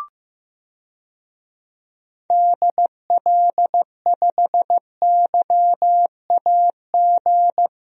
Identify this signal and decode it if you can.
Morse code, heard as a keyed tone, about 15 wpm, 700 Hz. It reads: DL5YAG
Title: DXCC-100 Call Signs - 6 Characters - Repeated 3x - 15wpm